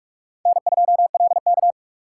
Morse code, heard as a keyed tone, about 46 wpm, 700 Hz. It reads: N2LK
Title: K1USN SST QSO Practice - 45wpm